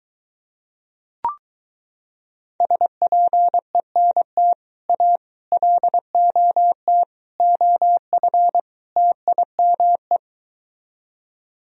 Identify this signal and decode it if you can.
Morse code, heard as a keyed tone, about 23 wpm, 700 Hz. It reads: SPENT A LOT OF TIME